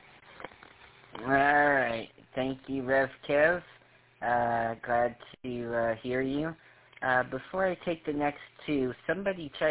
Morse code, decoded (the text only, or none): none